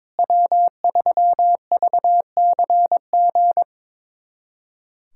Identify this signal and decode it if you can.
Morse code, heard as a keyed tone, about 22 wpm, 700 Hz. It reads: W3VCG
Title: USA Callsign DE Callsign 22wpm